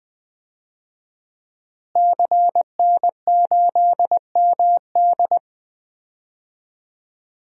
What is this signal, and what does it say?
Morse code, heard as a keyed tone, about 21 wpm, 700 Hz. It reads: CN8MD
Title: DXCC-100 Call Signs - 5 Characters - Repeated 2x - 20wpm